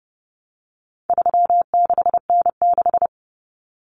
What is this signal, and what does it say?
Morse code, heard as a keyed tone, about 30 wpm, 700 Hz. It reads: 36N6
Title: Common Cut Numbers - 4-Digit 30wpm